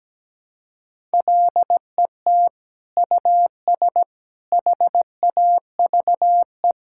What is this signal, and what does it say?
Morse code, heard as a keyed tone, about 17 wpm, 700 Hz. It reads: LET US HAVE
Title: Sentences from Top 1000 Words - Review Part 6 - 17wpm